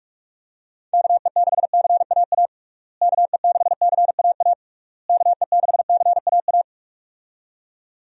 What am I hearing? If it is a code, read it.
Morse code, heard as a keyed tone, about 45 wpm, 700 Hz. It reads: KE6CAA KE6CAA KE6CAA
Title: USA Call Signs - 6 Characters - Repeated 3x - 45wpm